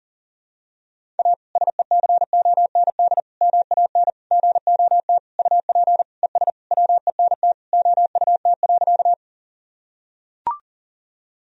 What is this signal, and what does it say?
Morse code, heard as a keyed tone, about 40 wpm, 700 Hz. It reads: A SECOND MAN GOT UP ES WENT OUT.
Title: Sentences from Top 300 Words and Abbreviations 40wpm